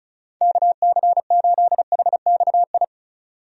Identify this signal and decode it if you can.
Morse code, heard as a keyed tone, about 36 wpm, 700 Hz. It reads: KC8HXI
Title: USA Call Signs - 6 Characters - 35wpm